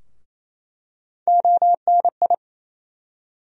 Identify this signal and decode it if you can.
Morse code, heard as a keyed tone, about 28 wpm, 700 Hz. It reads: ONI
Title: USA Call Sign Postfixes - 3 Letters 28wpm